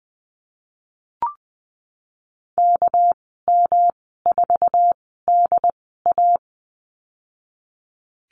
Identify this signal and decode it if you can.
Morse code, heard as a keyed tone, about 20 wpm, spaced as for 15 wpm, 700 Hz. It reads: KM4DA
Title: 231 Lesson 37g - Letter x - Call Signs 20wpm